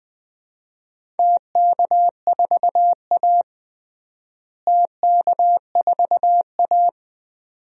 Morse code, heard as a keyed tone, about 20 wpm, 700 Hz. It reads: TK4A TK4A
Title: DXCC-100 Call Signs - 4 Characters - Repeated 2x - 20wpm